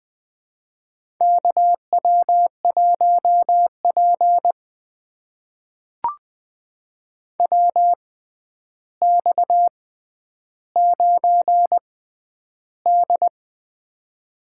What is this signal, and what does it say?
Morse code, heard as a keyed tone, about 20 wpm, spaced as for 10 wpm, 700 Hz. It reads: KW1P WX9D
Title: USA Call Signs - 4 Characters - 6x Character Spacing - 20wpm